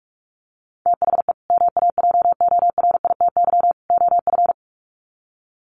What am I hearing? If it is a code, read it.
Morse code, heard as a keyed tone, about 44 wpm, 700 Hz. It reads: THE MAJORITY OF
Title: Sets of 3 Words 45wpm